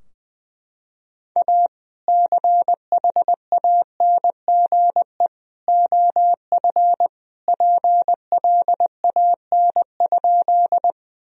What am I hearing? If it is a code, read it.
Morse code, heard as a keyed tone, about 20 wpm, 700 Hz. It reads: A CHANGE OF PLAN?
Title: Sentences from Top 1000 Words - Review Part 2 - 20wpm